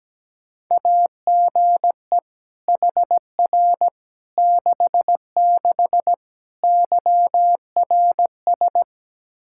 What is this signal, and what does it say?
Morse code, heard as a keyed tone, about 17 wpm, 700 Hz. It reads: AGE HR 66 YRS
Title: Practice QSO - Exchange 3 - More Info - 17wpm